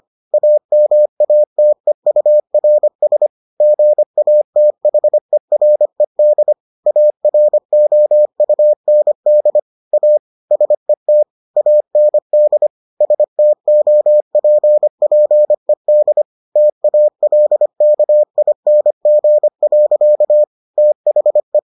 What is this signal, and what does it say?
Morse code, heard as a keyed tone, about 25 wpm, 600 Hz. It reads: AMATEURS GATHERED AROUND A SET AND STOPPED TALKING. THE